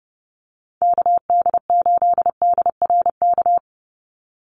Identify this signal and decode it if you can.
Morse code, heard as a keyed tone, about 30 wpm, 700 Hz. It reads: KD8DRK
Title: USA Call Signs 30wpm